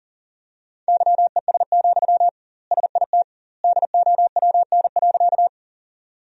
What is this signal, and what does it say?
Morse code, heard as a keyed tone, about 40 wpm, 700 Hz. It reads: YES, SIT DOWN.
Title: 2x Word-Spacing - Sentences from Top 500 Words - 40wpm